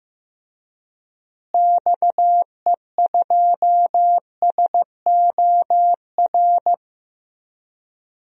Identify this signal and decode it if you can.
Morse code, heard as a keyed tone, about 15 wpm, 700 Hz. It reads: XE2SOR
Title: DXCC-100 Call Signs - 6 Characters - Repeated 2x - 15wpm